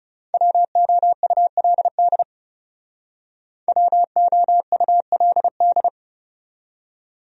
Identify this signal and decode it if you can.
Morse code, heard as a keyed tone, about 38 wpm, 700 Hz. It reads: WOULD WOULD